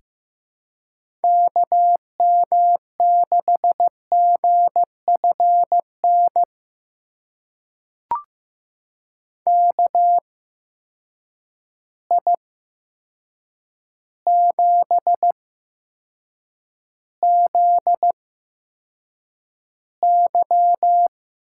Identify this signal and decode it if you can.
Morse code, heard as a keyed tone, about 15 wpm, spaced as for 7 wpm, 700 Hz. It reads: KM6GFN KI7ZY
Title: USA Call Signs - 6 Characters - 8x Character Spacing - 15wpm